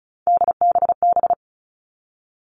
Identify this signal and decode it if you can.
Morse code, heard as a keyed tone, about 36 wpm, 700 Hz. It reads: DBB
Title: Three Challenging Letter-Number Combinations 35wpm